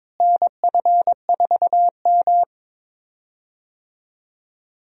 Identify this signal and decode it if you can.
Morse code, heard as a keyed tone, about 21 wpm, 700 Hz. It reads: NF4M
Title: USA Call Signs - 4 Characters - Repeated 2x - 22wpm